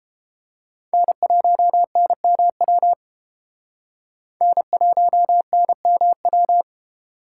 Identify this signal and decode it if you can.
Morse code, heard as a keyed tone, about 30 wpm, 700 Hz. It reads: N1NMW N1NMW